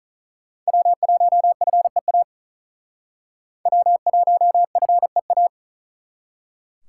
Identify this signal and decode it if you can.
Morse code, heard as a keyed tone, about 47 wpm, 700 Hz. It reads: W1FEA W1FEA